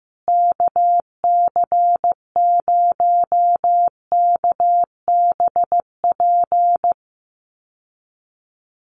Morse code, heard as a keyed tone, about 15 wpm, 700 Hz. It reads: KC0KBP
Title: USA Call Signs Repeated 3x 15wpm